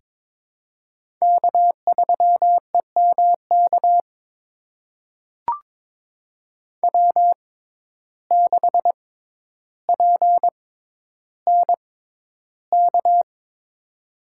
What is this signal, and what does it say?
Morse code, heard as a keyed tone, about 22 wpm, spaced as for 11 wpm, 700 Hz. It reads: K3EMK W6PNK
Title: USA Call Signs - 5 Characters - 6x Character Spacing - 22wpm